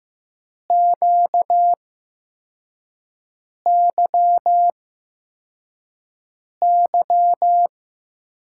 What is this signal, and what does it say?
Morse code, heard as a keyed tone, about 15 wpm, spaced as for 4 wpm, 700 Hz. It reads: QYY
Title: USA Call Signs - 5 Characters - 8x Character Spacing - 15wpm